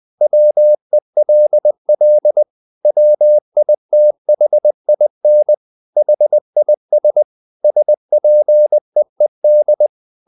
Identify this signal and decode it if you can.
Morse code, heard as a keyed tone, about 20 wpm, 600 Hz. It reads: WELL WITHIN HIS SPEED
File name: the old man and the sea / the old man and the sea part 1 20wpm